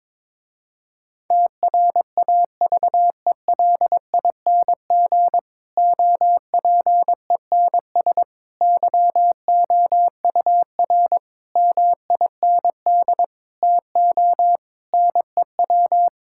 Morse code, heard as a keyed tone, about 22 wpm, 700 Hz. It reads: TRAVELING OPENS YOUR MIND TO NEW
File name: Difficult Sentences with Hints - 22wpm